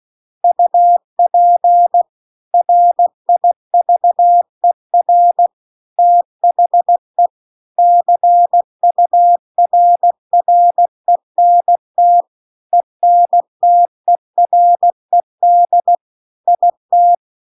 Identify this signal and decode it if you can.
Morse code, heard as a keyed tone, about 16 wpm, 700 Hz. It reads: UP RIVER THE CURRENT ENTERED IT